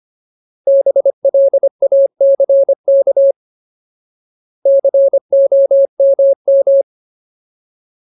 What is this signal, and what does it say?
Morse code, heard as a keyed tone, about 25 wpm, 550 Hz. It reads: BLACK COMM
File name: Ultimate Word List / 25WPM Ultimate